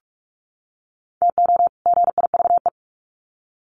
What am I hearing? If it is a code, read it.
Morse code, heard as a keyed tone, about 44 wpm, 700 Hz. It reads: TO GIVE
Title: Sets of 2 Words 45wpm